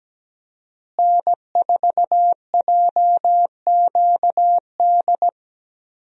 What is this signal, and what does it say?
Morse code, heard as a keyed tone, about 17 wpm, 700 Hz. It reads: N4JQD
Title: USA Call Signs - 5 Characters - Repeated 3x - 17wpm